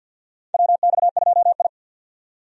Morse code, heard as a keyed tone, about 50 wpm, 700 Hz. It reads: WX2I
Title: USA Call Signs - 4 Characters - 50wpm